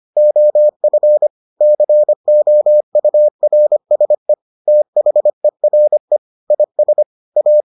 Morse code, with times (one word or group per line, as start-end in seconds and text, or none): 0.2-1.3 OF
1.6-4.3 COURSE
4.7-6.2 THERE
6.5-7.0 IS
7.4-7.6 A